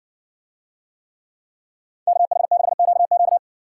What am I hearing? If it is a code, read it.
Morse code, heard as a keyed tone, about 60 wpm, 700 Hz. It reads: KH6QQ